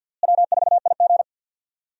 Morse code, heard as a keyed tone, about 46 wpm, 700 Hz. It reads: W4IG
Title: USA Call Signs - 4 Characters - Repeated 3x - 50wpm